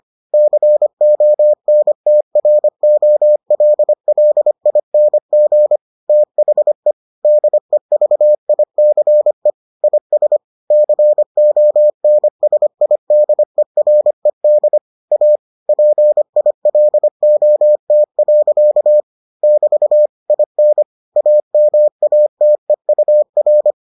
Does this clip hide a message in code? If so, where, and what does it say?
Yes: Morse code from 0.3 to 23.7 s, heard as a keyed tone, about 25 wpm, 600 Hz. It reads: CONTROLLING THE DEVICE IS CONSIDERED A PILOT. <BT> IN AMATEUR